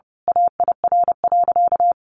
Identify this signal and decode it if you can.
Morse code, heard as a keyed tone, about 30 wpm, 700 Hz. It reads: AIR.
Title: A Sound Of Thunder 30wpm